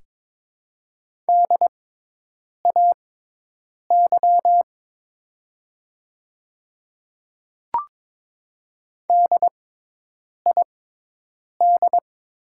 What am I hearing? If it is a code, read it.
Morse code, heard as a keyed tone, about 23 wpm, spaced as for 8 wpm, 700 Hz. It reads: DAY DID
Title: 6x Character Spacing - Top 200 Words - 22wpm